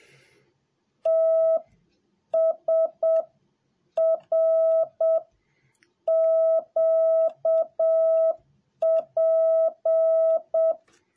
Morse code, heard as a keyed tone, about 7 wpm, 650 Hz. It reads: T S R Q P